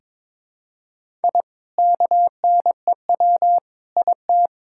Morse code, heard as a keyed tone, about 22 wpm, 700 Hz. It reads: I KNEW IT
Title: Sentences from Top 1000 Words - Review Part 6 - 22wpm